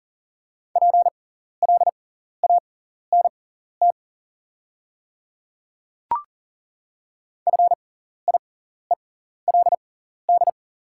Morse code, heard as a keyed tone, about 40 wpm, spaced as for 14 wpm, 690 Hz. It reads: PLANT FIELD
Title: 6x Character Spacing - Top 500 Words - 40wpm